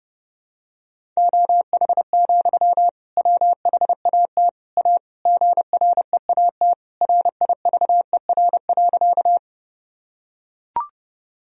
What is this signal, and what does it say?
Morse code, heard as a keyed tone, about 30 wpm, 700 Hz. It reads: OH, WHAT A GREAT RIVER.
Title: Sentences from Top 1000 Words - Review Part 1 - 30wpm